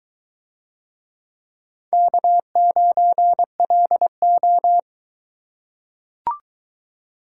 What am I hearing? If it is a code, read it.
Morse code, heard as a keyed tone, about 21 wpm, 700 Hz. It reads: K9LO